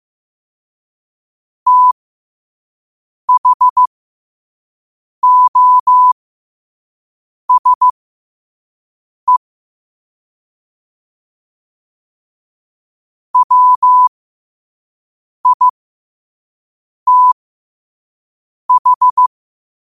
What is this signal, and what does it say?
Morse code, heard as a keyed tone, about 15 wpm, spaced as for 5 wpm, 1000 Hz. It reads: THOSE WITH
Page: Quotes for Wed, 13 Aug 2025 in Morse Code at 5 words per minute.